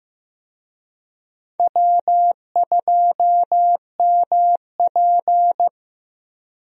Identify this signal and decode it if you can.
Morse code, heard as a keyed tone, about 16 wpm, 700 Hz. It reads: W2MP
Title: USA Call Signs - 4 Characters - Repeated 3x - 15wpm